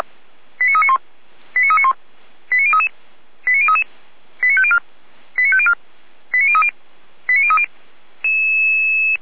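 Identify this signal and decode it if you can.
Five-tone selective call (ZVEI), heard as quick runs of five tones, tones of about 70 ms. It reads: 89391 89391 89033 89033 89504 89504 89030 89030
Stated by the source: ZVEI five-tone selective call received on SDR.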